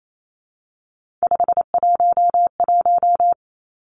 Morse code, heard as a keyed tone, about 28 wpm, 700 Hz. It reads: 511
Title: Three Challenging Letter-Number Combinations 28wpm